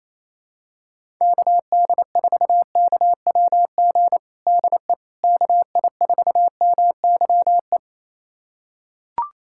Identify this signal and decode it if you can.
Morse code, heard as a keyed tone, about 28 wpm, 700 Hz. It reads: KD4KWG DE KI4MYE
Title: USA Callsign DE Callsign 28wpm